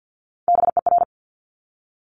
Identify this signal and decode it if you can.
Morse code, heard as a keyed tone, about 50 wpm, 700 Hz. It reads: BER
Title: Top Three Letter Combinations in Words 50wpm